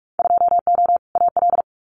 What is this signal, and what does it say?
Morse code, heard as a keyed tone, about 46 wpm, 700 Hz. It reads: 2O AL
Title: Winter Field Day Exchanges 45wpm